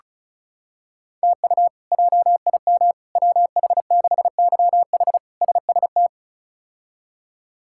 Morse code, heard as a keyed tone, about 34 wpm, 700 Hz. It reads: TU JIM WH6YH SST
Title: K1USN SST QSO Practice - 35wpm